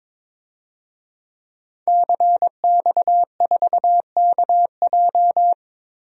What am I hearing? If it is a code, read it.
Morse code, heard as a keyed tone, about 22 wpm, 700 Hz. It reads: CX4KJ
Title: DXCC-100 Call Signs - 5 Characters - 22wpm